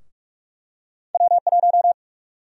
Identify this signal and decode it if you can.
Morse code, heard as a keyed tone, about 43 wpm, 700 Hz. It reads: W1
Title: USA Call Sign Prefixes - Repeated 2x - 45wpm